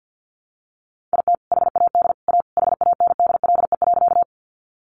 Morse code, heard as a keyed tone, about 50 wpm, 700 Hz. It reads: IT HAD A HANDLE.
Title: Sentences from Top 1100 Words 50wpm